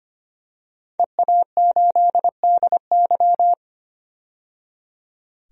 Morse code, heard as a keyed tone, about 25 wpm, 700 Hz. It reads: EA8DY
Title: DXCC-100 Call Signs - 5 Characters - 25wpm